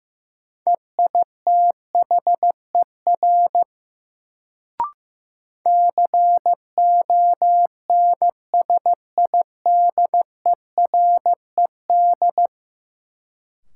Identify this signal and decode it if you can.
Morse code, heard as a keyed tone, about 15 wpm, 700 Hz. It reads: EITHER CONSIDERED
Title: Sets of 5 Words - New Word Drill - 15wpm